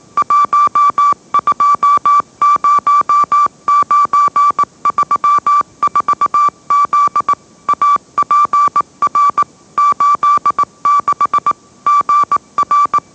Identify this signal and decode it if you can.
Morse code, heard as a keyed tone, about 18 wpm, 1160 Hz. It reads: 120934Z APR 86 GR
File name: mrx / mrx082 20wpm